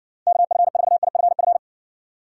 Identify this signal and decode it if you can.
Morse code, heard as a keyed tone, about 60 wpm, 690 Hz. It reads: KF4IFF